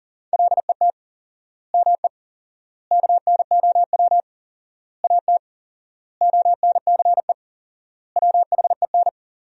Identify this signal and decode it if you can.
Morse code, heard as a keyed tone, about 37 wpm, 700 Hz. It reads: LET ME KNOW AT ONCE WHEN
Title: 4x Word-Spacing - Sentences from Top 300 Words - 40wpm